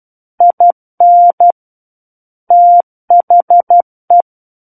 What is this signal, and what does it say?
Morse code, heard as a keyed tone, about 12 wpm, 700 Hz. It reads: IN THE
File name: Book One / 03-Chapter 3 - 12 WPM